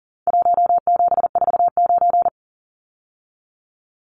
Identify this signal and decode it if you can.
Morse code, heard as a keyed tone, about 40 wpm, 700 Hz. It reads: 1749
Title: Numbers - Contest Style - 4-Digit 40wpm